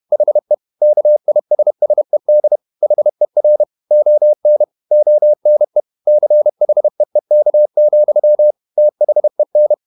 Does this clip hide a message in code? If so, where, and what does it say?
Yes: Morse code from 0.1 to 9.7 s, heard as a keyed tone, about 31 wpm, 600 Hz. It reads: HE KISSED HER ON ONE CHEEK, THEN